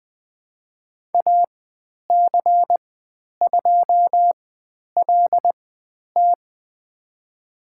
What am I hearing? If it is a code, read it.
Morse code, heard as a keyed tone, about 20 wpm, spaced as for 10 wpm, 700 Hz. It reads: AC2LT
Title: 095 Lesson 16g - Letter c - Call Signs 20wpm